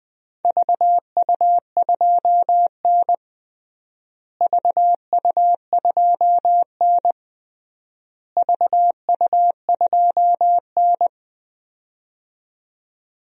Morse code, heard as a keyed tone, about 20 wpm, 700 Hz. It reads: VU2N VU2N VU2N
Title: DXCC-100 Call Signs - 4 Characters - Repeated 3x - 20wpm